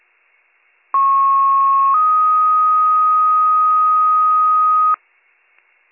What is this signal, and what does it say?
Two-tone page, heard as a one-second tone, then a three-second tone, about 1090 then 1250 Hz, with noise. It is heard from 0.9 to 4.9 s.